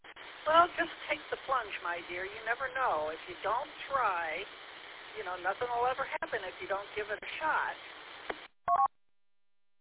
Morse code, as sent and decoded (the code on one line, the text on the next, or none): none